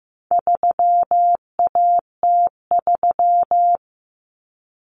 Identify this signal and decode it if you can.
Morse code, heard as a keyed tone, about 15 wpm, 700 Hz. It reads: 3AT3
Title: Common Cut Numbers - 4-Digit 15wpm